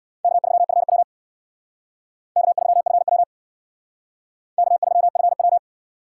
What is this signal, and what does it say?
Morse code, heard as a keyed tone, about 62 wpm, 700 Hz. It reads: K3FW K3FW K3FW
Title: Speed Racing - USA Call Signs 50wpm